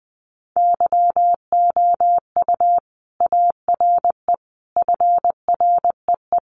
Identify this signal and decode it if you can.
Morse code, heard as a keyed tone, about 20 wpm, 700 Hz. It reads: YOU ARE FREE
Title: Sentences from Top 500 Words 20wpm